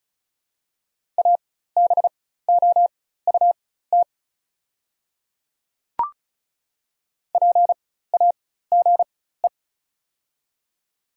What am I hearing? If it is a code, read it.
Morse code, heard as a keyed tone, about 36 wpm, spaced as for 16 wpm, 700 Hz. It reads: ABOUT PAGE
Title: 4x Character Spacing - Top 300 Words - 35wpm